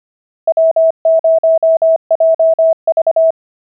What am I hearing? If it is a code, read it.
Morse code, heard as a keyed tone, about 25 wpm, 650 Hz. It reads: W0JV